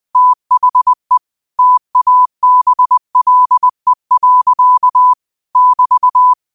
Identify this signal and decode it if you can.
Morse code, heard as a keyed tone, about 20 wpm, 1000 Hz. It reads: THE TABLE. <BT>
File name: qotdfiles / 20wpm